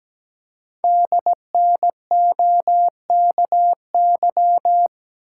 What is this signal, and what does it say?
Morse code, heard as a keyed tone, about 17 wpm, 700 Hz. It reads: DNOKY
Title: Code Groups - 5 Letters - 17wpm